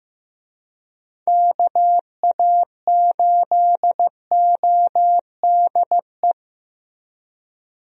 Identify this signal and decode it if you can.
Morse code, heard as a keyed tone, about 15 wpm, 700 Hz. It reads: KA8ODE
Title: USA Call Signs - 6 Characters - Repeated 2x - 15wpm